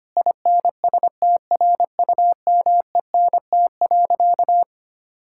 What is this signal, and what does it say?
Morse code, heard as a keyed tone, about 25 wpm, 700 Hz. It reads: INSTRUMENT.
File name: practice / 25wpm / Difficult Sentences with Hints - 25wpm